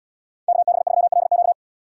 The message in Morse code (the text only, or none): KB3LQ